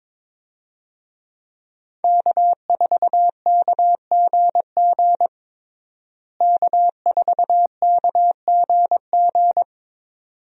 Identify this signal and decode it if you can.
Morse code, heard as a keyed tone, about 22 wpm, 700 Hz. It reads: K4KGG K4KGG
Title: USA Call Signs - 5 Characters - Repeated 2x - 22wpm